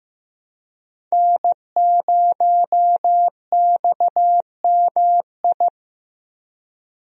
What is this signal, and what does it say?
Morse code, heard as a keyed tone, about 15 wpm, 700 Hz. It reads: N0XMI
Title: USA Call Signs - 5 Characters - Repeated 2x - 15wpm